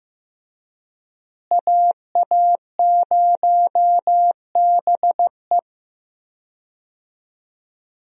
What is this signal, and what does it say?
Morse code, heard as a keyed tone, about 15 wpm, 700 Hz. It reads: AA0BE
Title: USA Call Signs - 5 Characters - Repeated 3x - 15wpm